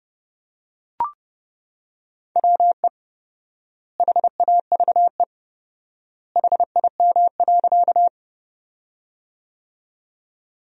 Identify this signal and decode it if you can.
Morse code, heard as a keyed tone, about 32 wpm, 700 Hz. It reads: WE HAVE HIM.
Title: 4x Word-Spacing - Sentences from Top 100 Words - 30wpm